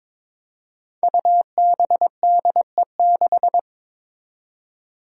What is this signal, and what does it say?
Morse code, heard as a keyed tone, about 22 wpm, 700 Hz. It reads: UBDE6
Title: Five Challenging Letter-Number Combinations 22wpm